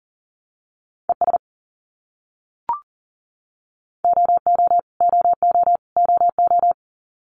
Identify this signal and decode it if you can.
Morse code, heard as a keyed tone, about 40 wpm, 700 Hz. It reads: ES OO OO OO
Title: Abbreviations Repeated 3x - 40wpm